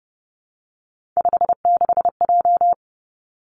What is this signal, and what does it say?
Morse code, heard as a keyed tone, about 30 wpm, 700 Hz. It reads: 56J
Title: Three Challenging Letter-Number Combinations 30wpm